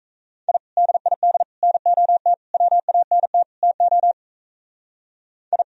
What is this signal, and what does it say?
Morse code, heard as a keyed tone, about 34 wpm, 700 Hz. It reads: I DID NOT WANT TO I
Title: Speed-Racing - Sets of 5 Words 28wpm